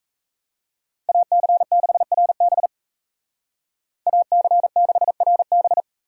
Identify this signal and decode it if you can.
Morse code, heard as a keyed tone, about 34 wpm, 700 Hz. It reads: AC6RB AC6RB
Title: Speed Racing - USA 2x2 Call Signs 28wpm